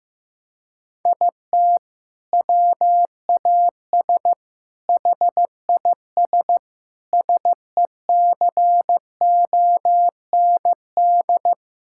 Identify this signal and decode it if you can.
Morse code, heard as a keyed tone, about 15 wpm, 700 Hz. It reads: IT WAS HIS SECOND
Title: Sentences from Top 1600 Words 15wpm